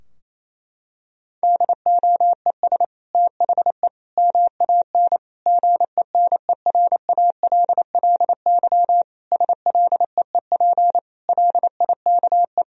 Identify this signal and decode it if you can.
Morse code, heard as a keyed tone, about 28 wpm, 700 Hz. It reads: DOES THE MAN GENERALLY SLEEP LIKE
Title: Sentences from Top 1000 Words - Review Part 4 - 28wpm